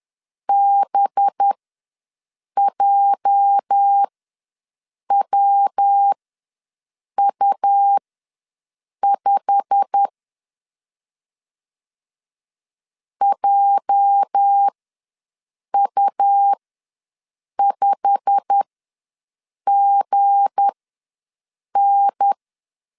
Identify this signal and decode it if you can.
Morse code, heard as a keyed tone, about 10 wpm, spaced as for 6 wpm, 800 Hz. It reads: BJWU5 JU5GN
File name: LEZIONI / 8 YJ45 / 8 YJ45 ws 10wpm 3843